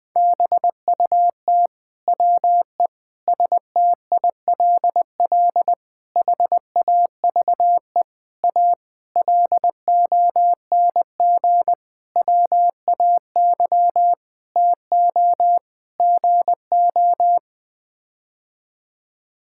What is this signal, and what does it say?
Morse code, heard as a keyed tone, about 20 wpm, 700 Hz. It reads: BUT WE STILL HAVE A LONG WAY TO GO
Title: Sets of 9 Words - 20wpm